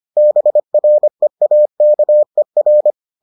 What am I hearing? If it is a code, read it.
Morse code, heard as a keyed tone, about 25 wpm, 600 Hz. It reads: BREAKER